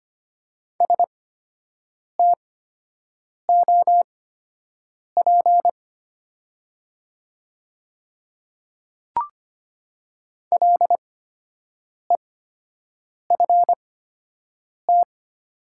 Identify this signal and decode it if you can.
Morse code, heard as a keyed tone, about 25 wpm, spaced as for 7 wpm, 700 Hz. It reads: STOP LEFT
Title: 8x Character Spacing - Top 300 Words - 25wpm